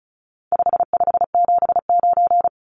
Morse code, heard as a keyed tone, about 34 wpm, 700 Hz. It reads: 5579
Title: Numbers - 4-Digit 35wpm